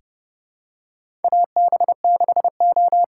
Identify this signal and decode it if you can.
Morse code, heard as a keyed tone, about 30 wpm, 700 Hz. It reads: AB6O
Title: USA Call Signs - 5 Characters - 8x Character Spacing - 30wpm